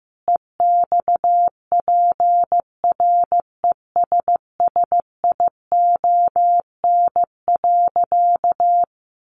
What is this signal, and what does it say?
Morse code, heard as a keyed tone, about 15 wpm, 700 Hz. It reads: EXPRESSION.
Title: Sentences from Top 1200 Words 15wpm